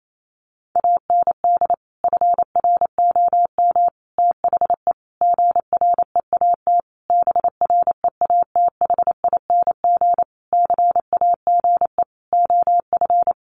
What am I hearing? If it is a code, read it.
Morse code, heard as a keyed tone, about 28 wpm, 700 Hz. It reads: AND FROM THE GREAT BREATHING CAGE OF
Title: A Sound Of Thunder 28wpm